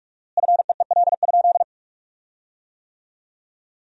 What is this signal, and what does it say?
Morse code, heard as a keyed tone, about 41 wpm, 700 Hz. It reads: FEEL?
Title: 4x Word-Spacing - Sentences from Top 400 Words - 45wpm